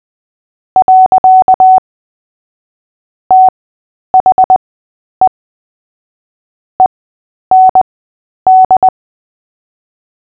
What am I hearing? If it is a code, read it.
Morse code, heard as a keyed tone, about 20 wpm, spaced as for 10 wpm, 750 Hz. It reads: . THE END
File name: Short-Story-5-10wpm